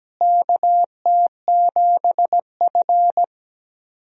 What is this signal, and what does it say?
Morse code, heard as a keyed tone, about 17 wpm, 700 Hz. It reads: KT7F
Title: USA Call Signs - 4 Characters - Repeated 2x - 17wpm